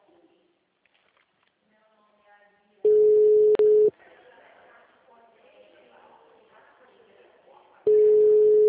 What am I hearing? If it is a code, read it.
Ringback tone (Europe, 425 Hz), with speech.